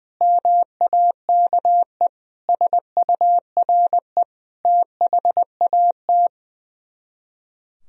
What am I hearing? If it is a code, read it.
Morse code, heard as a keyed tone, about 21 wpm, 700 Hz. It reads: MAKE SURE THAT
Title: Speed-Racing - Sets of 5 Words 20wpm